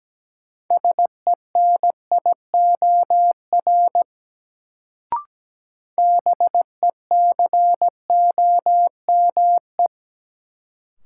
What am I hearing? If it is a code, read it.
Morse code, heard as a keyed tone, about 17 wpm, 700 Hz. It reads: SENIOR BECOME
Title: Sets of 5 Words Encore - New Word Drill - 17wpm